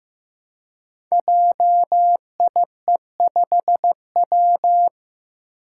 Morse code, heard as a keyed tone, about 15 wpm, 700 Hz. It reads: JIE5W
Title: Code Groups - 5 Letter:Numbers - 15wpm